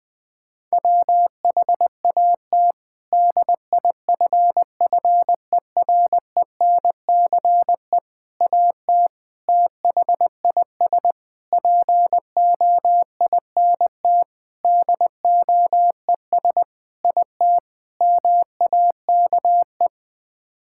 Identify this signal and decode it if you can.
Morse code, heard as a keyed tone, about 21 wpm, 700 Hz. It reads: WHAT DIFFERENCE AT THIS POINT DOES IT MAKE
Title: Sets of 8 Words - 20wpm